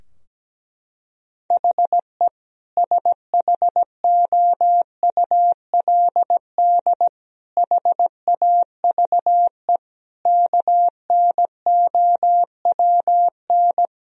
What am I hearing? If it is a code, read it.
Morse code, heard as a keyed tone, about 17 wpm, 700 Hz. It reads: HE SHOULD HAVE KNOWN
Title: Sentences from Top 1000 Words - Review Part 3 - 17wpm